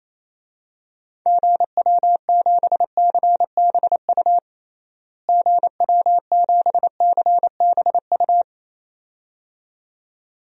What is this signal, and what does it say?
Morse code, heard as a keyed tone, about 28 wpm, 700 Hz. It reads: GW7CBU GW7CBU
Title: DXCC-100 Call Signs - 6 Characters - Repeated 2x - 28wpm